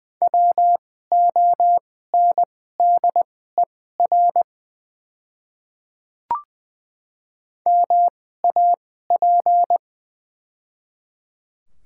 Morse code, heard as a keyed tone, about 20 wpm, spaced as for 14 wpm, 690 Hz. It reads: WONDER MAP
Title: 2x Character Spacing - Top 500 Words - 20wpm